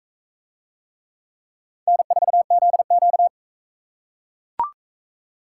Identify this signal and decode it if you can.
Morse code, heard as a keyed tone, about 40 wpm, 700 Hz. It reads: N4ZQ